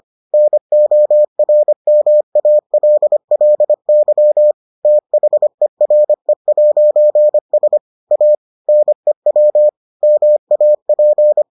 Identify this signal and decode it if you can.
Morse code, heard as a keyed tone, about 25 wpm, 600 Hz. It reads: NORMALLY THERE'S A NEW MAP